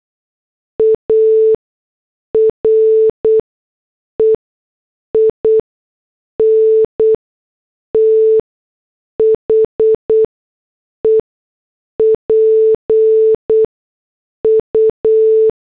Morse code, heard as a keyed tone, about 8 wpm, spaced as for 6 wpm, 440 Hz. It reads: ARE IN THE PU